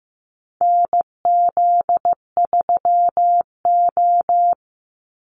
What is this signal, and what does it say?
Morse code, heard as a keyed tone, about 15 wpm, 700 Hz. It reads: NZ3O
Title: Call Signs Repeated 3x - 15wpm